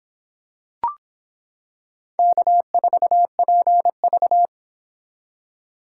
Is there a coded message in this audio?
Yes: Morse code from 2.2 to 4.5 s, heard as a keyed tone, about 25 wpm, 700 Hz. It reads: K4PV